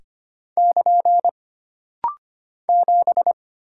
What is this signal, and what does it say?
Morse code, heard as a keyed tone, about 25 wpm, 700 Hz. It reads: <KN> 7
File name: Single Character (Letter:Number:Punctuation:Prosign) - Rapid-Fire - 25wpm